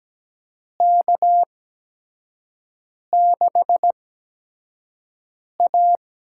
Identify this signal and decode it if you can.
Morse code, heard as a keyed tone, about 17 wpm, spaced as for 5 wpm, 700 Hz. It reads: K6A
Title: USA Call Signs - 4 Characters - 8x Character Spacing - 17wpm